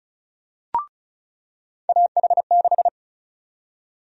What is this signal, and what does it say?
Morse code, heard as a keyed tone, about 36 wpm, 700 Hz. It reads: AH6
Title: USA Call Sign Prefixes - 35wpm